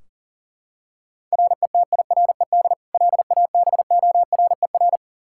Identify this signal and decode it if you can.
Morse code, heard as a keyed tone, about 40 wpm, 700 Hz. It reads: RETIRED LABORER
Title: Practice QSO - Exchange 3 - More Info - 40wpm